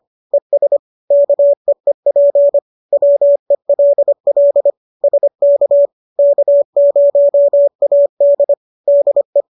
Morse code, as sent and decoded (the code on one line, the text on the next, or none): . ... / -.- . .--. / .-- . .-.. .-.. / ... -.- / -.- ----- .- -.. / -.. .
ES KEEP WELL SK K0AD DE